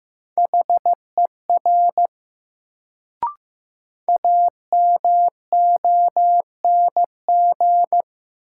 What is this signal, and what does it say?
Morse code, heard as a keyed tone, about 15 wpm, 700 Hz. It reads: HER AMONG
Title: Sets of 2 Words - New Word Drill - 15wpm